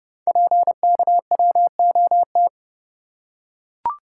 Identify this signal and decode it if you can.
Morse code, heard as a keyed tone, about 30 wpm, 700 Hz. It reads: PKWOT
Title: Code Groups - 5 Letters - 30wpm